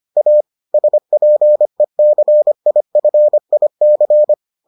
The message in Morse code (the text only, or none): A SPECIFIC